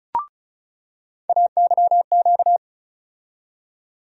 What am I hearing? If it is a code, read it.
Morse code, heard as a keyed tone, about 36 wpm, 700 Hz. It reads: AYQ
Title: USA Call Sign Postfixes - 3 Letters 35wpm